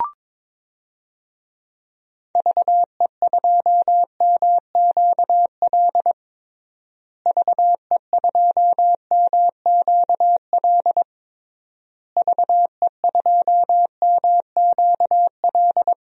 Morse code, heard as a keyed tone, about 22 wpm, 700 Hz. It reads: VE2MQL VE2MQL VE2MQL
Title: DXCC-100 Call Signs - 6 Characters - Repeated 3x - 22wpm